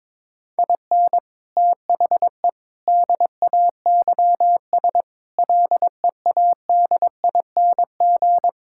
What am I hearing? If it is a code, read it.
Morse code, heard as a keyed tone, about 22 wpm, 700 Hz. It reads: IN THE DAYS LEADING
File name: practice / 22wpm / Sets of 7 Words - 22wpm